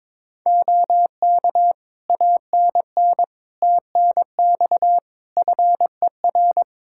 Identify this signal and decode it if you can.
Morse code, heard as a keyed tone, about 22 wpm, 700 Hz. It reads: OK ANN TNX FER
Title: Practice QSO - Exchange 4 - Ending - 22wpm